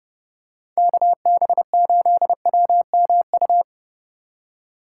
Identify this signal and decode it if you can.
Morse code, heard as a keyed tone, about 27 wpm, 700 Hz. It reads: KB8WMU